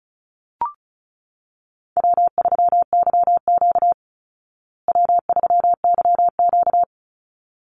Morse code, heard as a keyed tone, about 34 wpm, 700 Hz. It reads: W3YQ W3YQ
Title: Easy Call Signs Repeated 2x - 35wpm